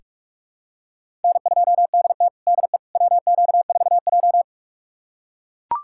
Morse code, heard as a keyed tone, about 42 wpm, 700 Hz. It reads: N2DT DE WQ4J
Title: USA Callsign DE Callsign - 4 Characters - 45wpm